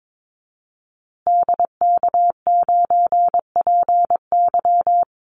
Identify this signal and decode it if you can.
Morse code, heard as a keyed tone, about 22 wpm, 700 Hz. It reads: DK9PY
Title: Call Signs Repeated 2x - 22wpm